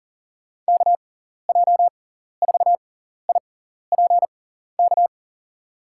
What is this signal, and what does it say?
Morse code, heard as a keyed tone, about 40 wpm, spaced as for 14 wpm, 700 Hz. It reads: KJ4IPK
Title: USA Call Signs - 6x Character Spacing - 40wpm